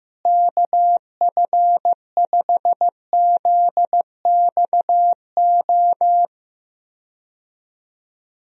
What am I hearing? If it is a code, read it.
Morse code, heard as a keyed tone, about 15 wpm, 700 Hz. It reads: KF5ZXO
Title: USA Call Signs - 6 Characters - Repeated 3x - 15wpm